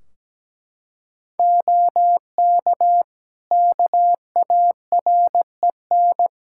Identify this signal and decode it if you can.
Morse code, heard as a keyed tone, about 17 wpm, 700 Hz. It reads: OK KAREN
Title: Practice QSO - Exchange 3 - More Info - 17wpm